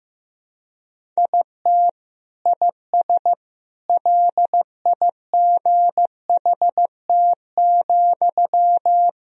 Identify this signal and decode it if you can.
Morse code, heard as a keyed tone, about 16 wpm, 700 Hz. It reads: IT IS LIGHT,
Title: Sentences from Top 1000 Words - Review Part 1 - 15wpm